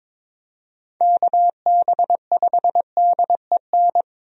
Morse code, heard as a keyed tone, about 21 wpm, 700 Hz. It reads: KB5DEN
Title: USA Call Signs - 6 Characters - Repeated 2x - 22wpm